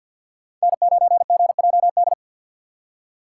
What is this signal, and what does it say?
Morse code, heard as a keyed tone, about 50 wpm, 700 Hz. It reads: N9GJD
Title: USA Call Signs - 5 Characters - 4x Character Spacing - 50wpm